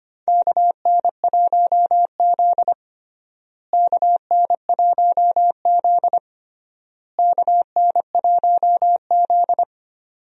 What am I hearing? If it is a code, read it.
Morse code, heard as a keyed tone, about 25 wpm, 700 Hz. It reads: KN1Z KN1Z KN1Z
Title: USA Call Signs - 4 Characters - Repeated 3x - 25wpm